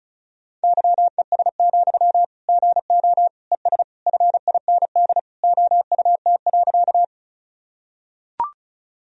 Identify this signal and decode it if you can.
Morse code, heard as a keyed tone, about 36 wpm, 700 Hz. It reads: YES, GO ES FIND OUT.
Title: Sentences from Top 500 Words and Abbreviations 35wpm